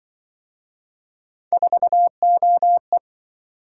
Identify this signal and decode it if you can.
Morse code, heard as a keyed tone, about 27 wpm, 700 Hz. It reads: 4OE